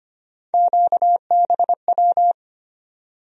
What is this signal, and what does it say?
Morse code, heard as a keyed tone, about 25 wpm, 700 Hz. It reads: QBW
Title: USA Call Sign Postfixes - 3 Letters 25wpm